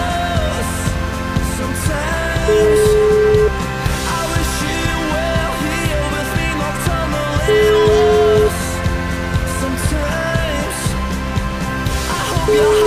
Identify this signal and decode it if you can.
Ringback tone (Europe, 425 Hz), with noise.